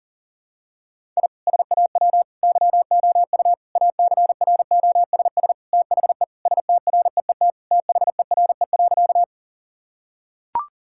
Morse code, heard as a keyed tone, about 40 wpm, 700 Hz. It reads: I SAW YOU ACROSS THE STREET THERE.
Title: Sentences from Top 1000 Words - Review Part 5 - 40wpm